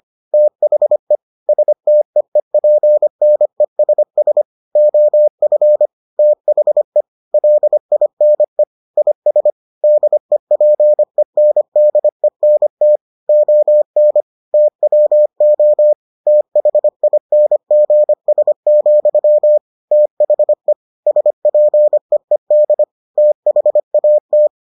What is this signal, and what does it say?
Morse code, heard as a keyed tone, about 25 wpm, 600 Hz. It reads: THE STEEPNESS OF THE LINE IS DEPENDENT ON TWO THINGS, THE SPEED THAT